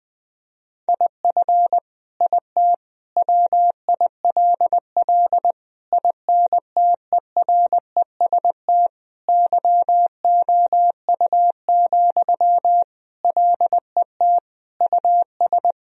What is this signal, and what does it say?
Morse code, heard as a keyed tone, about 20 wpm, 700 Hz. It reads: IF IT WILL INTEREST YOU, LET US
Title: Sentences from Top 1000 Words - Review Part 2 - 20wpm